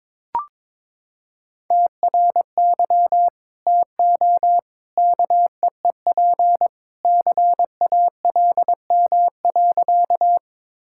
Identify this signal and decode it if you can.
Morse code, heard as a keyed tone, about 22 wpm, 700 Hz. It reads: TRY TO KEEP CALM.
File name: practice / 22wpm / Sentences with 2-Word Phrases - 22wpm